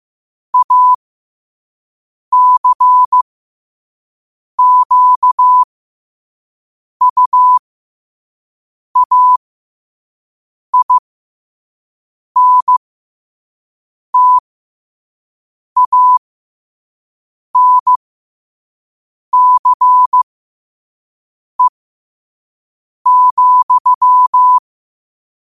Morse code, heard as a keyed tone, about 15 wpm, spaced as for 5 wpm, 1000 Hz. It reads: ACQUAINTANCE,
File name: qotdfiles / 5wpm